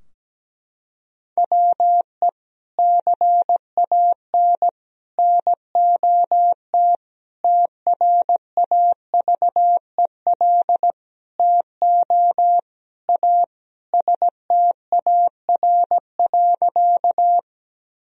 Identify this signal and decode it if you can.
Morse code, heard as a keyed tone, about 17 wpm, 700 Hz. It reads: WE CAN NOT TRAVEL TO A STAR.
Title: Sentences from Top 1000 Words - Review Part 2 - 17wpm